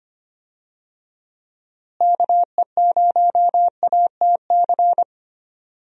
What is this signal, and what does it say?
Morse code, heard as a keyed tone, about 25 wpm, 700 Hz. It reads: KE0ATC
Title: USA Call Signs - 6 Characters - Repeated 3x - 25wpm